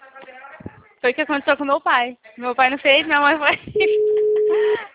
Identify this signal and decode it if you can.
Ringback tone (Europe, 425 Hz), with speech.